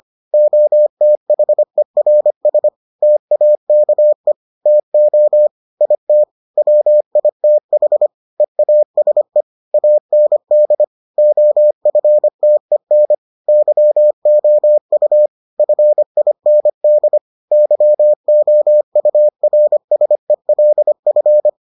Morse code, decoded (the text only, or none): OTHERS TAKE TO IT WITH EASE AND OFTEN YOU FIND YOURSELF